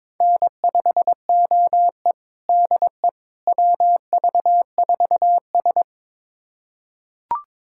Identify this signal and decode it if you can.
Morse code, heard as a keyed tone, about 22 wpm, 700 Hz. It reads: N5OE DE WV4S
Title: USA Callsign DE Callsign - 4 Characters - 22wpm